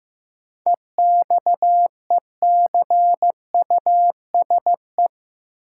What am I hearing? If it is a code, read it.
Morse code, heard as a keyed tone, about 15 wpm, 700 Hz. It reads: EXECUSE
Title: Sentences from Top 700 Words - New Word Drill - 15wpm